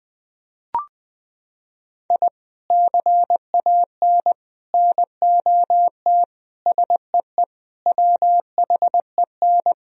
Morse code, heard as a keyed tone, about 20 wpm, 700 Hz. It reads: I CAN NOT SEE WHEN
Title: Sentences from Top 1000 Words - Review Part 4 - 20wpm